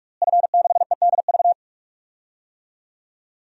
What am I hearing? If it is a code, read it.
Morse code, heard as a keyed tone, about 47 wpm, 700 Hz. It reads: F6EDV
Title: DXCC-100 Call Signs - 5 Characters - Repeated 2x - 45wpm